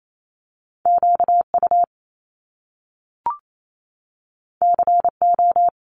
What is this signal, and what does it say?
Morse code, heard as a keyed tone, about 28 wpm, 700 Hz. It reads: QU CO
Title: Top Two Letter Combinations in Words 28wpm